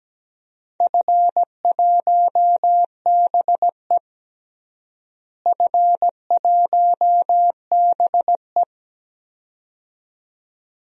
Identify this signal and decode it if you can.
Morse code, heard as a keyed tone, about 17 wpm, 700 Hz. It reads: F1BE F1BE
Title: DXCC-100 Call Signs - 4 Characters - Repeated 2x - 17wpm